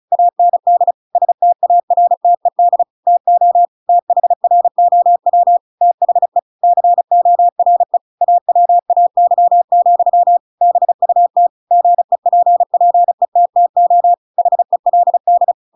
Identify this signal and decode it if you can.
Morse code, heard as a keyed tone, about 36 wpm, 700 Hz. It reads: AND STARTED TO THROW THE CORE AWAY, BUT GEPPETTO HELD
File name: Pinocchio / Pinocchio ch7 35 35 ik1hge